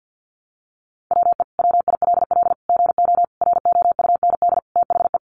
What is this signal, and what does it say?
Morse code, heard as a keyed tone, about 50 wpm, 700 Hz. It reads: WE WILL GO ROUND THE
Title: Sentences from Top 1300 Words 50wpm